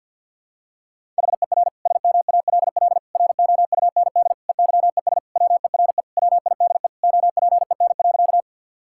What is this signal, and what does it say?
Morse code, heard as a keyed tone, about 52 wpm, 700 Hz. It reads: HER SMALL ROUND EYES WERE WIDE OPEN.
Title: Sentences from Top 1000 Words - Review Part 6 - 50wpm